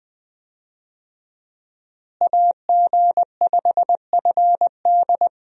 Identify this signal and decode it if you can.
Morse code, heard as a keyed tone, about 20 wpm, 700 Hz. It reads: AG5FD
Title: USA Call Signs - 5 Characters - Repeated 2x - 20wpm